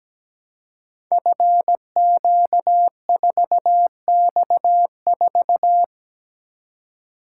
Morse code, heard as a keyed tone, about 17 wpm, 700 Hz. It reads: FQ4X4
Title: Code Groups - 5 Letter:Numbers - 17wpm